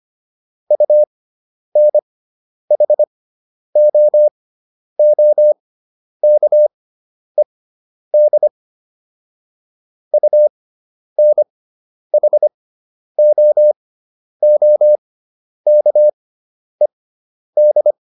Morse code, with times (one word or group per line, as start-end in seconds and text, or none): 0.7-8.5 UNHOOKED
10.1-17.9 UNHOOKED